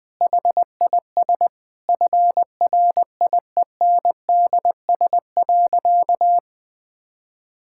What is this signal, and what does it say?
Morse code, heard as a keyed tone, about 20 wpm, 700 Hz. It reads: HIS FRIENDS.
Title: Sentences with 4-Word Phrases - Spoken Beginning - 20wpm